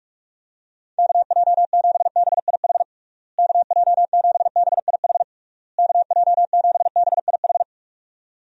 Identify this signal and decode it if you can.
Morse code, heard as a keyed tone, about 46 wpm, 700 Hz. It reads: KJ7BIH KJ7BIH KJ7BIH
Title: USA Call Signs - 6 Characters - Repeated 3x - 45wpm